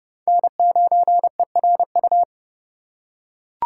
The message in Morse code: N9ERU

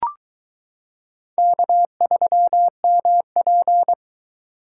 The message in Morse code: K3MP

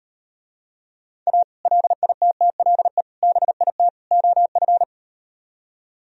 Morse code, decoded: A LITTLE BIT OF

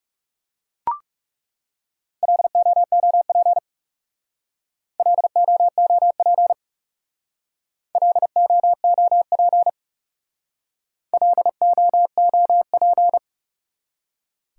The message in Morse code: LOOP LOOP LOOP LOOP

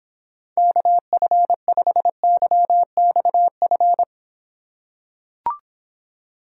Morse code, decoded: KF5YXF

